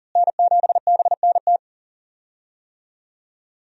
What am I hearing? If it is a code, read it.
Morse code, heard as a keyed tone, about 37 wpm, 700 Hz. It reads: N7BNT
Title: USA Call Signs - 5 Characters - Repeated 2x - 40wpm